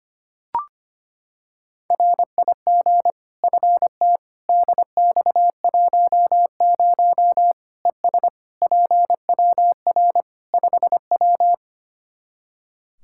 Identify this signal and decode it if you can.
Morse code, heard as a keyed tone, about 25 wpm, 700 Hz. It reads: RIG FT DX10 ES PWR 5W
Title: Practice QSO - Exchange 2 - Info - 25wpm